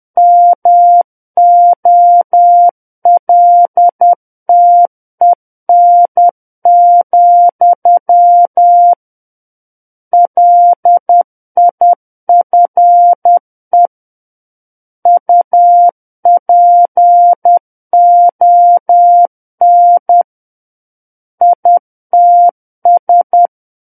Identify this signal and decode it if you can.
Morse code, heard as a keyed tone, about 10 wpm, 700 Hz. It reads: MOLTEN, LIFE UPON ITS
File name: Book One / 01-Chapter 1 - 10 WPM